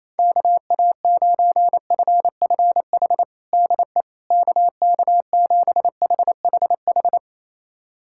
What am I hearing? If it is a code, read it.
Morse code, heard as a keyed tone, about 28 wpm, 700 Hz. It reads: KA9FFH DE KK7HHH
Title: USA Callsign DE Callsign - 6 Characters - 28wpm